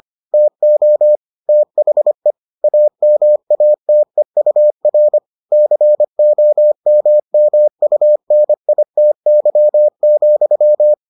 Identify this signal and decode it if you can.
Morse code, heard as a keyed tone, about 25 wpm, 600 Hz. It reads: TO THE AMATEUR COMMUNITY,